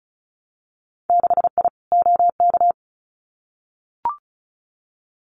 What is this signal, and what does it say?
Morse code, heard as a keyed tone, about 36 wpm, 700 Hz. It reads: 6I OK
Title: Winter Field Day Exchanges Encore 35wpm